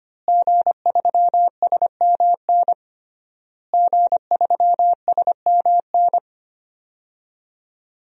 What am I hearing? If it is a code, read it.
Morse code, heard as a keyed tone, about 25 wpm, 700 Hz. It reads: G3SMN G3SMN
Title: DXCC-100 Call Signs - 5 Characters - Repeated 3x - 25wpm